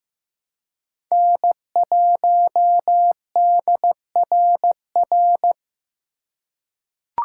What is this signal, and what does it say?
Morse code, heard as a keyed tone, about 15 wpm, 700 Hz. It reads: N1DRR
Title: USA Call Signs - 5 Characters - 2x Character Spacing - 15wpm